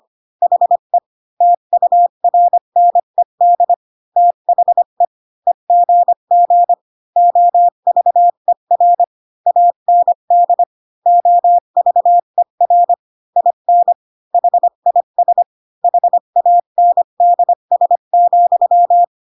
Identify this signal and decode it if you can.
Morse code, heard as a keyed tone, about 25 wpm, 700 Hz. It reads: HE TURNED THE EGG OVER AND OVER IN HIS HANDS,
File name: Pinocchio / Pinocchio ch5 20 25 ik1hge